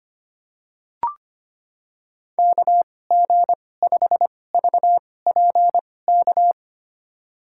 Morse code, heard as a keyed tone, about 25 wpm, spaced as for 18 wpm, 700 Hz. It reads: KG5VPK
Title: USA Call Signs - 2x Character Spacing - 25wpm